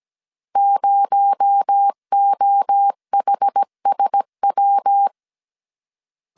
Morse code, heard as a keyed tone, about 17 wpm, 800 Hz. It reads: 0OHSW
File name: 16wpm